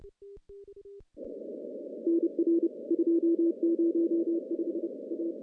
Morse code, heard as a keyed tone, about 30 wpm, 340 Hz. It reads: NR 205 A
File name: contests / CW-SS-k5tr- -1400